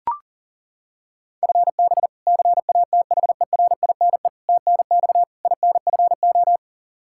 Morse code, heard as a keyed tone, about 37 wpm, 700 Hz. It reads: FB CATHERINE TNX INFO
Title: K1USN SST QSO Practice 40wpm